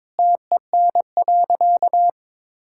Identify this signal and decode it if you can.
Morse code, heard as a keyed tone, about 22 wpm, 700 Hz. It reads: TEN.
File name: Sentences with 7-Word Phrases - 22wpm